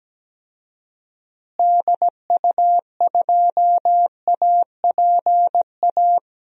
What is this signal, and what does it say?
Morse code, heard as a keyed tone, about 17 wpm, 700 Hz. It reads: DU2APA
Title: DXCC-100 Call Signs - 6 Characters - Repeated 3x - 17wpm